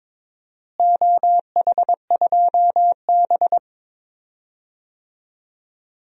Morse code, heard as a keyed tone, about 22 wpm, 700 Hz. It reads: OH2B
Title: DXCC-100 Call Signs - 4 Characters - Repeated 2x - 22wpm